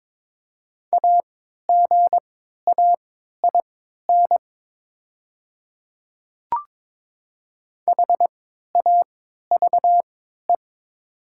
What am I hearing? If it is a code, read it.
Morse code, heard as a keyed tone, about 22 wpm, spaced as for 13 wpm, 700 Hz. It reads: AGAIN HAVE
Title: 3x Character Spacing - Top 300 Words - 22wpm